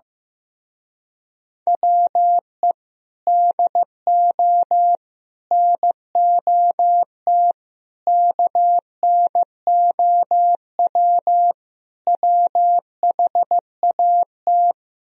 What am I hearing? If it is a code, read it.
Morse code, heard as a keyed tone, about 15 wpm, 700 Hz. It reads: WE DO NOT KNOW WHAT
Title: Speed-Racing - Sets of 5 Words 15wpm